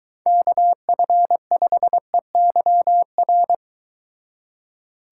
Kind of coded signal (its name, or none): Morse code